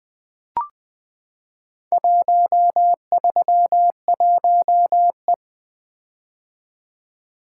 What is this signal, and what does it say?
Morse code, heard as a keyed tone, about 20 wpm, 700 Hz. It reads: 131E
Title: Four Letter-Number Combinations - 20wpm